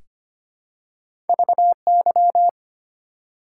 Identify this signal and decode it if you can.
Morse code, heard as a keyed tone, about 25 wpm, 700 Hz. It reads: VY
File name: Top 90 QSO Elements - 4x Character Spacing - 25wpm